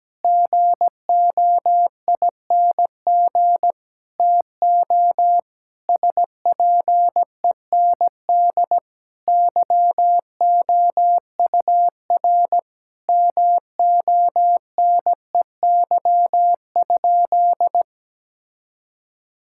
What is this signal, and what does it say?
Morse code, heard as a keyed tone, about 17 wpm, 700 Hz. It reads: GOING TO SPEND YOUR MONEY?
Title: Sentences from Top 1000 Words - Review Part 5 - 17wpm